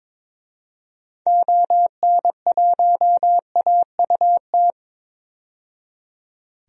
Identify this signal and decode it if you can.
Morse code, heard as a keyed tone, about 22 wpm, 700 Hz. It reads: ON1AUT
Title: DXCC-100 Call Signs - 6 Characters - 22wpm